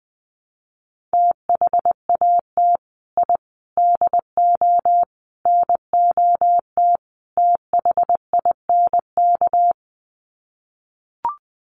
Morse code, heard as a keyed tone, about 20 wpm, 700 Hz. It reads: THAT I DO NOT THINK
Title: Sets of 5 Words Encore 20wpm